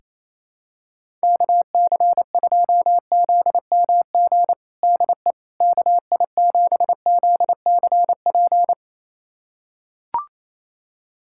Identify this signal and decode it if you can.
Morse code, heard as a keyed tone, about 28 wpm, 700 Hz. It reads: KC2ZMG DE KI7ZCP
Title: USA Callsign DE Callsign - 6 Characters - 28wpm